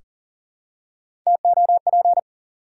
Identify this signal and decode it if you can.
Morse code, heard as a keyed tone, about 40 wpm, 700 Hz. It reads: TOP
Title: Three Letter Words - 40wpm